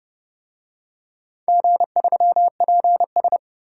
Morse code, heard as a keyed tone, about 30 wpm, 700 Hz. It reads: G3PS
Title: DXCC-100 Call Signs - 4 Characters - 30wpm